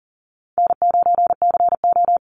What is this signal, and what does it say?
Morse code, heard as a keyed tone, about 40 wpm, 700 Hz. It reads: N9CO
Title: Easy Call Signs Repeated 3x - 40wpm